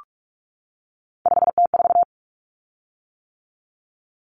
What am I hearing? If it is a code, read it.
Morse code, heard as a keyed tone, about 44 wpm, 700 Hz. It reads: 5T4
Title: Common Cut Numbers - 3-Digit 45wpm